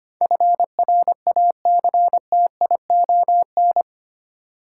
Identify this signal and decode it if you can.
Morse code, heard as a keyed tone, about 25 wpm, 700 Hz. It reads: FRACTION
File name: 8 Letter Words - First 3 Letters Given - 25wpm